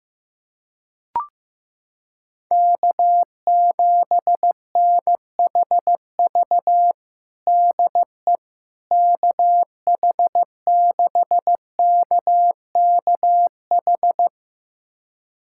Morse code, heard as a keyed tone, about 16 wpm, 700 Hz. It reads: K7NHV DE KH6KKH
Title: USA Callsign DE Callsign 15wpm